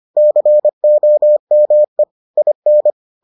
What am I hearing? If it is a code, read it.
Morse code, heard as a keyed tone, about 25 wpm, 600 Hz. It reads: COME IN